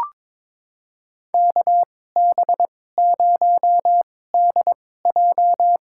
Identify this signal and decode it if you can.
Morse code, heard as a keyed tone, about 22 wpm, spaced as for 16 wpm, 700 Hz. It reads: KB0DJ
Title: USA Call Signs - 2x Character Spacing - 22wpm